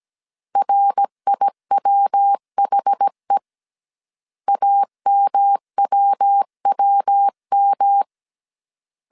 Morse code, heard as a keyed tone, about 17 wpm, 800 Hz. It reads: RIWHE AMWWM